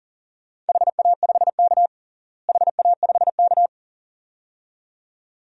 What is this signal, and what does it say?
Morse code, heard as a keyed tone, about 41 wpm, 700 Hz. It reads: HA5K HA5K
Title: DXCC-100 Call Signs - 4 Characters - Repeated 2x - 40wpm